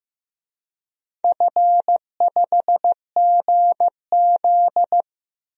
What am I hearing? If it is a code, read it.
Morse code, heard as a keyed tone, about 15 wpm, 700 Hz. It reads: F5GZ
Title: DXCC-100 Call Signs - 4 Characters - Repeated 3x - 15wpm